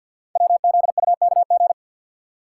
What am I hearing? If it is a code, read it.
Morse code, heard as a keyed tone, about 51 wpm, 700 Hz. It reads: WZUKG
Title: Code Groups - 5 Letters - 50wpm